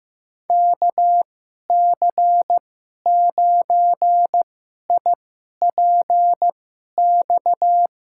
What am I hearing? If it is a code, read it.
Morse code, heard as a keyed tone, about 15 wpm, spaced as for 11 wpm, 700 Hz. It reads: KC9IPX
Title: USA Call Signs - 6 Characters - 2x Character Spacing - 15wpm